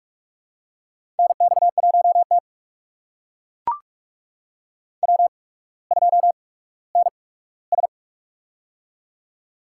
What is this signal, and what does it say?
Morse code, heard as a keyed tone, about 44 wpm, spaced as for 19 wpm, 700 Hz. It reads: NX1T W2NS
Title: USA Call Signs - 4 Characters - 8x Character Spacing - 45wpm